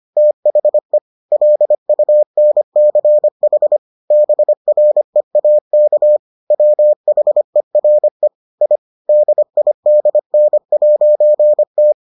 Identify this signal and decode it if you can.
Morse code, heard as a keyed tone, about 25 wpm, 600 Hz. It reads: THE LUNCH BREAK WHERE I DIDN'T